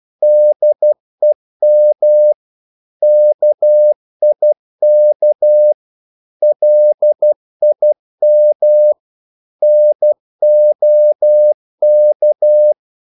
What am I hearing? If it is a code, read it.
Morse code, heard as a keyed tone, about 12 wpm, 600 Hz. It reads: DEM KIK LIM NOK
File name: Korte ord DK 12wpm